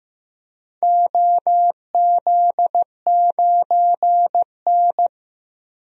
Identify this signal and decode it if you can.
Morse code, heard as a keyed tone, about 15 wpm, 700 Hz. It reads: OZ9N
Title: DXCC-100 Call Signs - 4 Characters - 15wpm